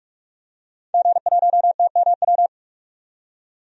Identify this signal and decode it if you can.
Morse code, heard as a keyed tone, about 43 wpm, 700 Hz. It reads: G1TMW
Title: DXCC-100 Call Signs - 5 Characters - 45wpm